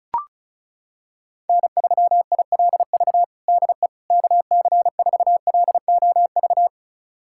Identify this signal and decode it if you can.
Morse code, heard as a keyed tone, about 36 wpm, 700 Hz. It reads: N3ILV DE KC4LOV
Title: USA Callsign DE Callsign 35wpm